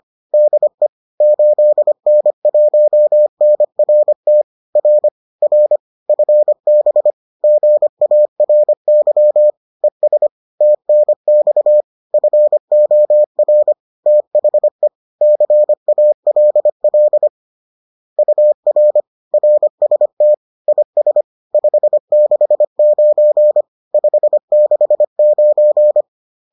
Morse code, heard as a keyed tone, about 25 wpm, 600 Hz. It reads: DE 8N1NRT R R FB GARY ES TNX FOR THE CALL UR RST IS 569 569